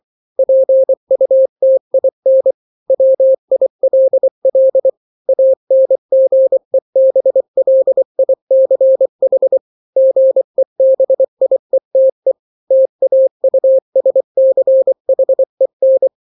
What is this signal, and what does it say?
Morse code, heard as a keyed tone, about 24 wpm, 530 Hz. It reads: PUTIN WILL ANGEBLICH GEBIETE TAUSCHEN
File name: daily / 23wpm